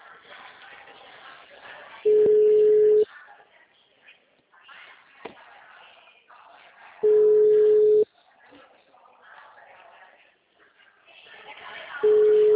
Ringback tone (Europe, 425 Hz).